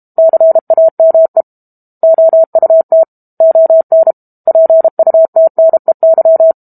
Morse code, heard as a keyed tone, about 32 wpm, 650 Hz. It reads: CAME OUT ON PUTNEY
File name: Book Two / 06-Chapter 6 - 32 WPM